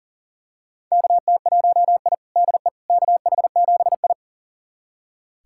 Morse code, heard as a keyed tone, about 40 wpm, 700 Hz. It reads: KT1I DE KH7I
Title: USA Callsign DE Callsign - 4 Characters - 40wpm